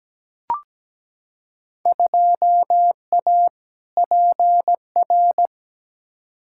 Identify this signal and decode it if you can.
Morse code, heard as a keyed tone, about 17 wpm, 700 Hz. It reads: 2A PR
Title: Field Day Exchanges Encore - 17wpm